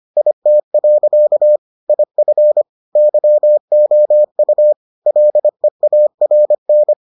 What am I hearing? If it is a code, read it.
Morse code, heard as a keyed tone, about 25 wpm, 600 Hz. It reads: IT. IF YOU LEARN